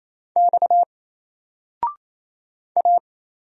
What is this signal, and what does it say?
Morse code, heard as a keyed tone, about 27 wpm, 700 Hz. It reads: X A
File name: Single-Letter-Number-28wpm